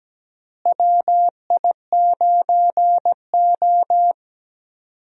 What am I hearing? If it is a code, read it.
Morse code, heard as a keyed tone, about 17 wpm, 700 Hz. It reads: WI9O
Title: USA Call Signs - 4 Characters - Repeated 2x - 17wpm